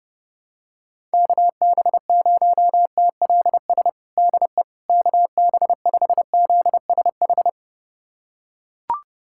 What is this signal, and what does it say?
Morse code, heard as a keyed tone, about 28 wpm, 700 Hz. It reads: KB0TLS DE KB5ZSH
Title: USA Callsign DE Callsign - 6 Characters - 30wpm